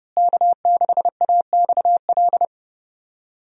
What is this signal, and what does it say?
Morse code, heard as a keyed tone, about 31 wpm, 700 Hz. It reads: K6AXL
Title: USA Call Signs - 5 Characters - Repeated 3x - 30wpm